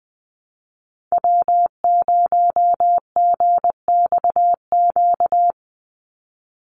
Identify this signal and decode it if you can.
Morse code, heard as a keyed tone, about 20 wpm, 700 Hz. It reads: W0GXQ
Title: Call Signs Repeated 3x - 20wpm